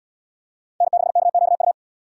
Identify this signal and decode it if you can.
Morse code, heard as a keyed tone, about 77 wpm, 700 Hz. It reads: N6KQU